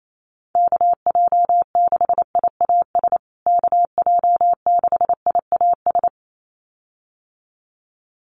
Morse code, heard as a keyed tone, about 28 wpm, 700 Hz. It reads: KJ6IAS KJ6IAS
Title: USA Call Signs Repeated 2x 28wpm